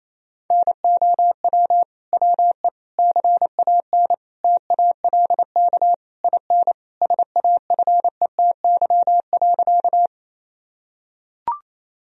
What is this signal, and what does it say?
Morse code, heard as a keyed tone, about 28 wpm, 700 Hz. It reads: NOW WE CAN TALK IN SAFETY.
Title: Sentences from Top 1000 Words - Review Part 4 - 28wpm